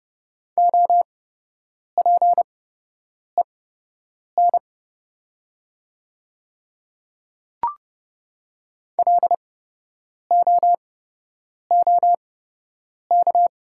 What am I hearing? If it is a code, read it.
Morse code, heard as a keyed tone, about 30 wpm, spaced as for 8 wpm, 700 Hz. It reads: OPEN LOOK
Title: 8x Character Spacing - Top 300 Words - 30wpm